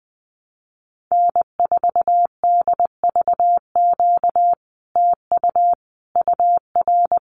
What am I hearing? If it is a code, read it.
Morse code, heard as a keyed tone, about 20 wpm, 700 Hz. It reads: N4DVQ TU UR
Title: Common QSO Phrases Encore 20wpm